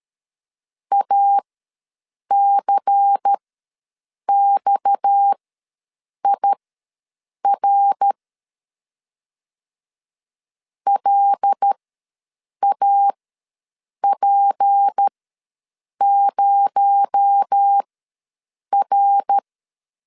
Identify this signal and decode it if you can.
Morse code, heard as a keyed tone, about 13 wpm, spaced as for 7 wpm, 800 Hz. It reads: ACXIR LAP0R
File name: LEZIONI / 6 PKX0 / 6 PKX0 ws 12wpm 3448